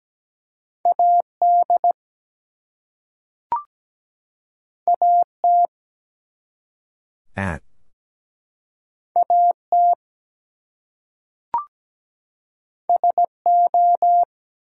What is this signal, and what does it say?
Morse code, heard as a keyed tone, about 17 wpm, 700 Hz. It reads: AD AT AT SO
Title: Two Letter Words - 17wpm